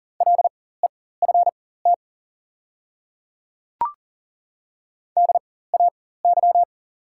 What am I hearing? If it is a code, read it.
Morse code, heard as a keyed tone, about 40 wpm, spaced as for 19 wpm, 700 Hz. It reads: LEFT DAY
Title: 4x Character Spacing - Top 300 Words - 40wpm